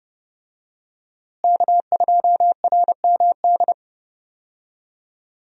Morse code, heard as a keyed tone, about 30 wpm, 700 Hz. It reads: K2RMD
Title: USA Call Signs - 5 Characters - Repeated 2x - 30wpm